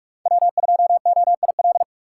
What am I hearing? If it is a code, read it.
Morse code, heard as a keyed tone, about 41 wpm, 700 Hz. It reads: W2OIL